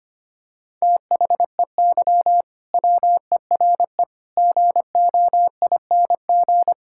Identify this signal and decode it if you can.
Morse code, heard as a keyed tone, about 25 wpm, 700 Hz. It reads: THEY WERE GOING
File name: practice / 25wpm / Sentences with 4-Word Phrases - 25wpm